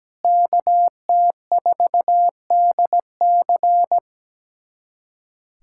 Morse code, heard as a keyed tone, about 17 wpm, 700 Hz. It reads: KT4DC
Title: USA Callsign DE Callsign - 5 Characters - 17wpm